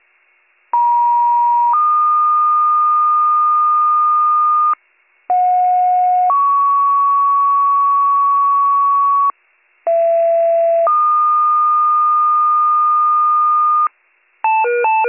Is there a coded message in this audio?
Yes: two-tone page from 0.7 to 13.9 s, heard as a one-second tone, then a three-second tone, about 950 then 1220 Hz, about 730 then 1120 Hz, about 670 then 1220 Hz.